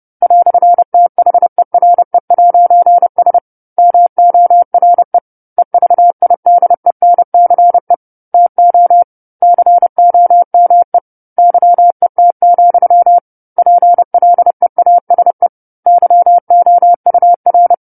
Morse code, heard as a keyed tone, about 30 wpm, 700 Hz. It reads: "THERE'S MORE EVIDENCE TO COME YET, PLEASE YOUR
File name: 10-Chapter-10 30 WPM